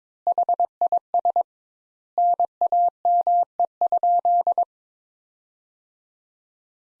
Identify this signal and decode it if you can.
Morse code, heard as a keyed tone, about 22 wpm, 700 Hz. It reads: HIS NAME?
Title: Sentences Top 200 Words - 22wpm